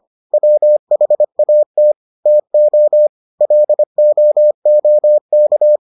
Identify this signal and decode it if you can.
Morse code, heard as a keyed tone, about 25 wpm, 600 Hz. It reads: WHAT TO LOOK